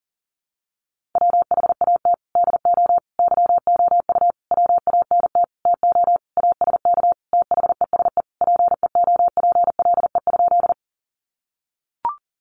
Morse code, heard as a keyed tone, about 40 wpm, 700 Hz. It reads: WHAT DO YOU WANT TO ASK THESE PEOPLE?
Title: Sentences from Top 200 Words 40wpm